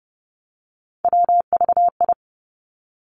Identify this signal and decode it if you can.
Morse code, heard as a keyed tone, about 30 wpm, 700 Hz. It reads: WVI
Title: Three Challenging Letter-Number Combinations 30wpm